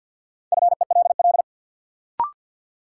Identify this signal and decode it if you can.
Morse code, heard as a keyed tone, about 51 wpm, 700 Hz. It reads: FELL
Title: Sentences from Top 1600 Words - New Word Drill - 50wpm